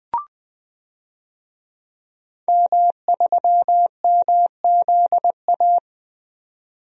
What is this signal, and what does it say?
Morse code, heard as a keyed tone, about 20 wpm, 700 Hz. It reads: M3MZA
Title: DXCC-100 Call Signs - 5 Characters - 20wpm